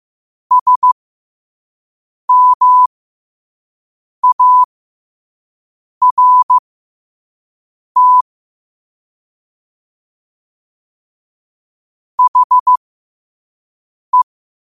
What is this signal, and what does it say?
Morse code, heard as a keyed tone, about 15 wpm, spaced as for 5 wpm, 1000 Hz. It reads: SMART HE